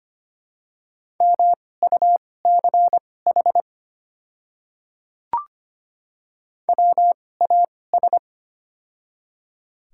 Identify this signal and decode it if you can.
Morse code, heard as a keyed tone, about 25 wpm, spaced as for 18 wpm, 700 Hz. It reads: MUCH WAS
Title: 2x Character Spacing - Top 500 Words - 25wpm